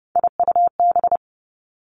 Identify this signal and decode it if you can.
Morse code, heard as a keyed tone, about 31 wpm, 700 Hz. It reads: IUB
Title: Three Challenging Letter-Number Combinations 30wpm